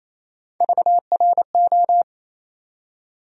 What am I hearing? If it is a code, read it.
Morse code, heard as a keyed tone, about 28 wpm, 700 Hz. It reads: VRO
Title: USA Call Sign Postfixes - 3 Letters 28wpm